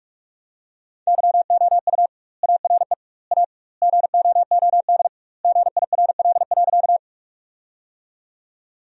Morse code, heard as a keyed tone, about 44 wpm, 700 Hz. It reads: YOU ARE A GOOD GIRL.
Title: 2x Word-Spacing - Sentences from Top 400 Words - 45wpm